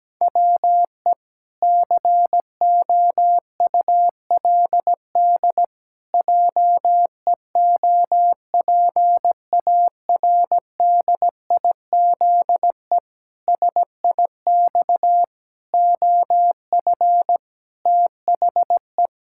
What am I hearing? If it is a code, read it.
Morse code, heard as a keyed tone, about 17 wpm, 700 Hz. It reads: WE COULD JEOPARDIZE SIX OF THE
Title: Pangram Challenge v2 - 17wpm